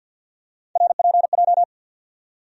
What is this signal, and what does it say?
Morse code, heard as a keyed tone, about 53 wpm, 700 Hz. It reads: RPJ
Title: USA Call Sign Postfixes - 3 Letters 50wpm